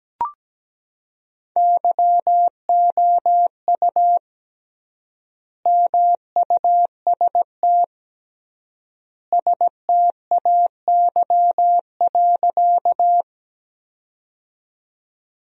Morse code, heard as a keyed tone, about 17 wpm, 700 Hz. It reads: YOU MUST STAY.
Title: 3x Word-Spacing - Sentences from Top 500 Words - 17wpm